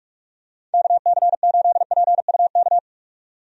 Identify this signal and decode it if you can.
Morse code, heard as a keyed tone, about 46 wpm, 700 Hz. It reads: KC8PUK
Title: USA Call Signs - 6 Characters - 45wpm